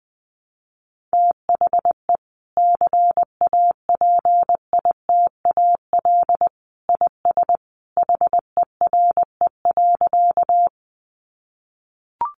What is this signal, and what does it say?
Morse code, heard as a keyed tone, about 20 wpm, 700 Hz. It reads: THE CAPITAL IS HERE.
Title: Sentences from Top 1000 Words 20wpm